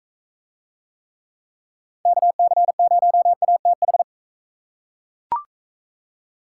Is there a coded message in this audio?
Yes: Morse code from 2.1 to 4.0 s, heard as a keyed tone, about 40 wpm, 700 Hz. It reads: KC0ATH